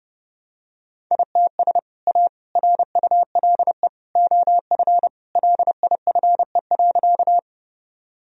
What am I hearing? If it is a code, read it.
Morse code, heard as a keyed tone, about 30 wpm, 700 Hz. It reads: ITS A RULE OF LIFE.
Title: Sentences from Top 1000 Words - Review Part 2 - 30wpm